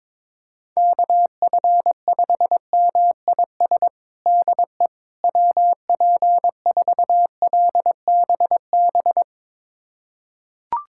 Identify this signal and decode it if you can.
Morse code, heard as a keyed tone, about 22 wpm, 700 Hz. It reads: KF5MIS DE WP4LBB
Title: USA Callsign DE Callsign - 6 Characters - 22wpm